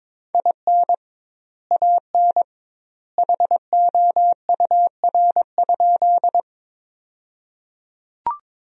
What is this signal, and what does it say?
Morse code, heard as a keyed tone, about 21 wpm, 700 Hz. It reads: IN AN HOUR?
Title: 2x Word-Spacing - Sentences from Top 400 Words - 22wpm